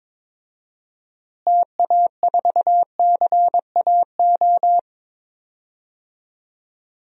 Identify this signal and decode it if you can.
Morse code, heard as a keyed tone, about 22 wpm, 700 Hz. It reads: TA4CAO
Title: DXCC-100 Call Signs - 6 Characters - Repeated 2x - 22wpm